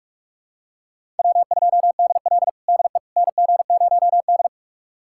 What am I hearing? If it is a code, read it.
Morse code, heard as a keyed tone, about 45 wpm, 700 Hz. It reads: W2DL DE NG0D
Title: USA Callsign DE Callsign - 4 Characters - 45wpm